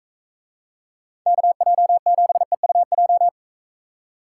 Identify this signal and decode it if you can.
Morse code, heard as a keyed tone, about 41 wpm, 700 Hz. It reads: KJ7EUJ